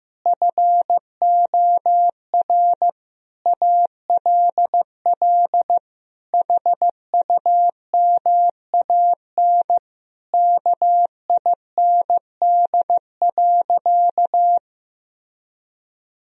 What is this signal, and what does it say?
Morse code, heard as a keyed tone, about 16 wpm, 700 Hz. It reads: FOR ALL HUMAN KIND.
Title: Sentences from Top 1000 Words - Review Part 4 - 15wpm